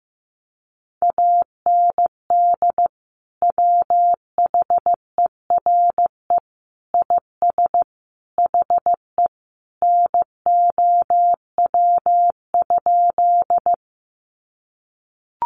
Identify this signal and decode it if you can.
Morse code, heard as a keyed tone, about 15 wpm, 700 Hz. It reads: AND WHERE IS HE NOW?
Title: Sentences from Top 200 Words 15wpm